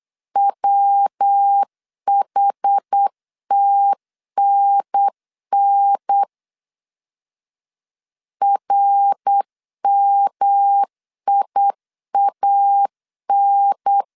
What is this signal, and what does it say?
Morse code, heard as a keyed tone, about 8 wpm, 800 Hz. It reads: WHTNN RMIAN
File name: CW / DET / 1 3 1500C S A 8wpm 4229